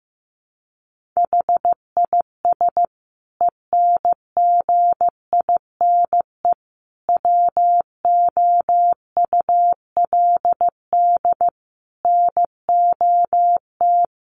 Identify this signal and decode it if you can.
Morse code, heard as a keyed tone, about 15 wpm, 700 Hz. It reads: HIS ENGINE WOULD NOT
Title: Sentences from Top 1500 Words 15wpm